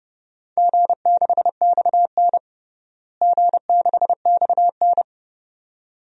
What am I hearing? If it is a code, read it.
Morse code, heard as a keyed tone, about 30 wpm, 700 Hz. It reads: G6XN G6XN
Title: DXCC-100 Call Signs - 4 Characters - Repeated 2x - 30wpm